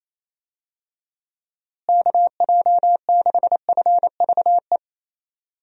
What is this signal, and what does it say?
Morse code, heard as a keyed tone, about 28 wpm, 700 Hz. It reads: KJ6FVE
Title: USA Call Signs - 6 Characters - Repeated 3x - 28wpm